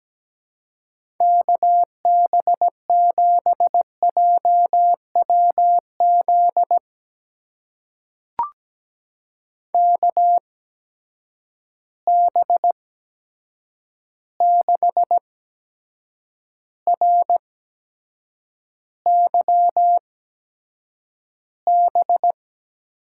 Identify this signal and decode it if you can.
Morse code, heard as a keyed tone, about 17 wpm, spaced as for 7 wpm, 700 Hz. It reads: KB7JWZ KB6RYB
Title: USA Call Signs - 6 Characters - 8x Character Spacing - 17wpm